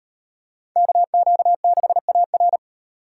Morse code, heard as a keyed tone, about 35 wpm, 700 Hz. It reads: KQ6AR